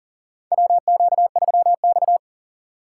Morse code, heard as a keyed tone, about 40 wpm, 700 Hz. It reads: WQ3X
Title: USA Call Signs - 4 Characters - 4x Character Spacing - 40wpm